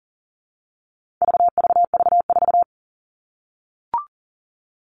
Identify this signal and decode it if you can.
Morse code, heard as a keyed tone, about 40 wpm, 700 Hz. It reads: VVV4
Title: All Cut Numbers - 4-Digit 40wpm